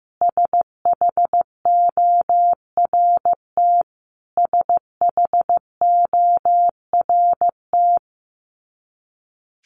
Morse code, heard as a keyed tone, about 15 wpm, 700 Hz. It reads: SHORT SHORT
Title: Top 500 Words Repeated 15wpm